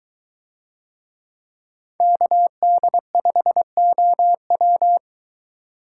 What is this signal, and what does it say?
Morse code, heard as a keyed tone, about 22 wpm, 700 Hz. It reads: KD5OW